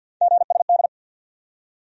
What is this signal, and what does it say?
Morse code, heard as a keyed tone, about 53 wpm, 700 Hz. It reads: GSD
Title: USA Call Sign Postfixes - 3 Letters 50wpm